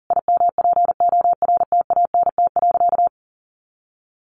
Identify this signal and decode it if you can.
Morse code, heard as a keyed tone, about 40 wpm, 700 Hz. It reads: IMPORTANT.
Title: Sentences from Top 600 Words 40wpm